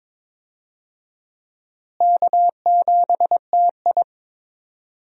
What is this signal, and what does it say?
Morse code, heard as a keyed tone, about 22 wpm, 700 Hz. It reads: K7TI
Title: USA Call Signs - 4 Characters - Repeated 3x - 22wpm